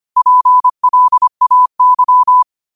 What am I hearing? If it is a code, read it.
Morse code, heard as a keyed tone, about 25 wpm, 1000 Hz. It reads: PLAY